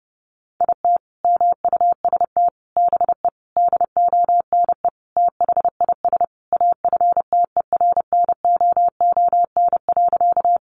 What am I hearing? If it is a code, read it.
Morse code, heard as a keyed tone, about 30 wpm, 700 Hz. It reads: IT MUST BE DONE THIS AFTERNOON.
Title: Sentences from Top 1500 Words 30wpm